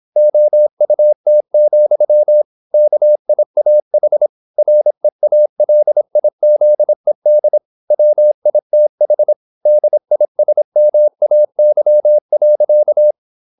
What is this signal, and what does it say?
Morse code, heard as a keyed tone, about 26 wpm, 600 Hz. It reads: OUT, KIAH REALIZED WITH DISMAY.
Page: Never 26wpm | CW med Gnister